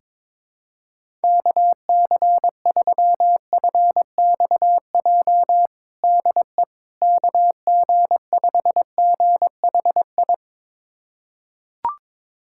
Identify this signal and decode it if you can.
Morse code, heard as a keyed tone, about 21 wpm, 700 Hz. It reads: KC3FXJ DE KG5GHI
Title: USA Callsign DE Callsign 22wpm